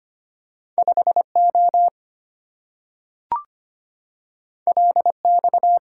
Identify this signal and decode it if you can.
Morse code, heard as a keyed tone, about 25 wpm, 690 Hz. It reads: 5O LX
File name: Two Letter-Number Combinations - 25wpm